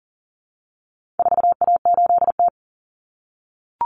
Morse code, heard as a keyed tone, about 40 wpm, 700 Hz. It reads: 4A8T
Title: Common Cut Numbers - 4-Digit 40wpm